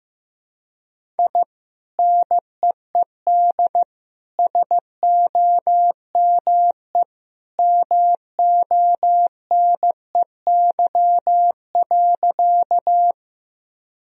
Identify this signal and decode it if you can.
Morse code, heard as a keyed tone, about 15 wpm, 700 Hz. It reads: I NEED SOME MONEY.
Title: Sentences from Top 1000 Words - Review Part 1 - 15wpm